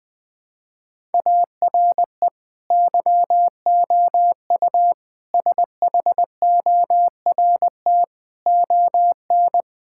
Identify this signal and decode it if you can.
Morse code, heard as a keyed tone, about 20 wpm, 700 Hz. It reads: ARE YOU SHORT ON
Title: Sentences from Top 1000 Words - Review Part 5 - 20wpm